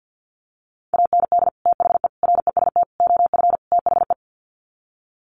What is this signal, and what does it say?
Morse code, heard as a keyed tone, about 50 wpm, 700 Hz. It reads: AND THE REST OF THE
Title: Sets of 5 Words 50wpm